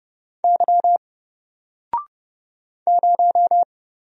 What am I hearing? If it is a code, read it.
Morse code, heard as a keyed tone, about 31 wpm, 700 Hz. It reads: Y 0
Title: Single Letter-Number Alt 30wpm